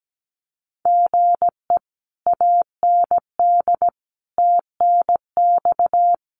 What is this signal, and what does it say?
Morse code, heard as a keyed tone, about 17 wpm, 700 Hz. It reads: GE AND TNX
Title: Common QSO Phrases Encore 17wpm